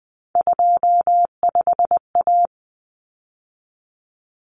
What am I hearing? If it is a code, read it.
Morse code, heard as a keyed tone, about 20 wpm, 700 Hz. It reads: 25A
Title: Common Cut Numbers - 3-Digit 20wpm